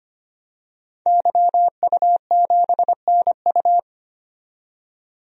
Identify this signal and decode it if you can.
Morse code, heard as a keyed tone, about 25 wpm, 700 Hz. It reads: YU7NU
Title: DXCC-100 Call Signs - 5 Characters - 25wpm